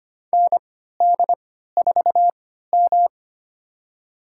Morse code, heard as a keyed tone, about 25 wpm, spaced as for 14 wpm, 700 Hz. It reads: ND4M
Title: USA Call Signs - 3x Character Spacing - 25wpm